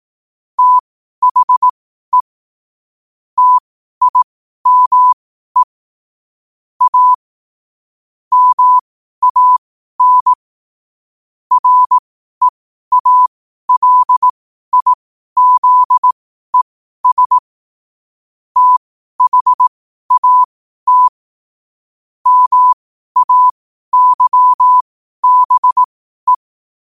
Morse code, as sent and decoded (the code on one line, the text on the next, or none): - .... / - .. -- . / .- / -- .- -. / .-. .- .-.. .. --.. ... / - .... .- - / -- .- -.-- -... .
THE TIME A MAN REALIZES THAT MAYBE